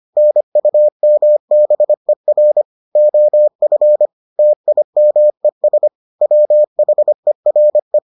Morse code, heard as a keyed tone, about 25 wpm, 600 Hz. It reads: NUMBER OF TIMES WHERE